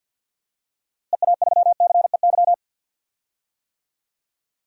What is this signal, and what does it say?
Morse code, heard as a keyed tone, about 51 wpm, 700 Hz. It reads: EA3CEY
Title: DXCC-100 Call Signs - 6 Characters - Repeated 2x - 50wpm